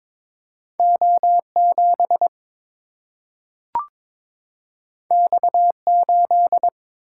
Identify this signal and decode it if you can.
Morse code, heard as a keyed tone, about 22 wpm, 700 Hz. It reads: O7 X8
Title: Single-Letter Single-Number 22wpm